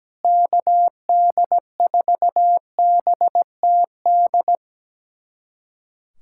Morse code, heard as a keyed tone, about 17 wpm, 700 Hz. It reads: KD4BTD
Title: USA Callsign DE Callsign - 6 Characters - 17wpm